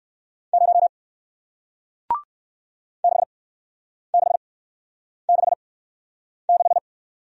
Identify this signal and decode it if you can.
Morse code, heard as a keyed tone, about 58 wpm, spaced as for 9 wpm, 700 Hz. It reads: 0 6666